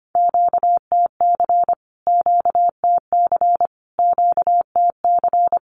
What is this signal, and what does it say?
Morse code, heard as a keyed tone, about 25 wpm, 700 Hz. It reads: QTC QTC QTC
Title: Q-Codes Repeated 3x 25wpm